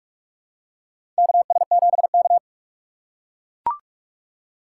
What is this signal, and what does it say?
Morse code, heard as a keyed tone, about 42 wpm, 700 Hz. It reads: KS7K
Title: USA Call Signs - 4 Characters - 4x Character Spacing - 45wpm